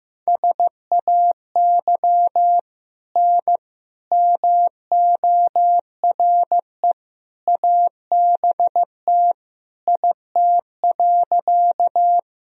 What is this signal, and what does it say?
Morse code, heard as a keyed tone, about 16 wpm, 700 Hz. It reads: SAY N MORE ABT IT.
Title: Sentences from Top 200 Words and Abbreviations 15wpm